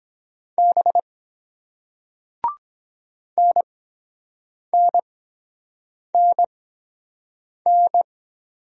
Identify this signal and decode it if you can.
Morse code, heard as a keyed tone, about 23 wpm, spaced as for 7 wpm, 700 Hz. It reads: B NNNN